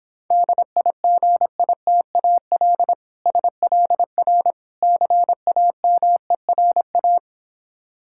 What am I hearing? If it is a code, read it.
Morse code, heard as a keyed tone, about 26 wpm, 700 Hz. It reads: DIGITAL SLR CAMERA